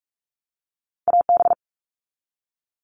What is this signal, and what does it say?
Morse code, heard as a keyed tone, about 44 wpm, 700 Hz. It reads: AB
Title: All Cut Numbers - 2-Digit 45wpm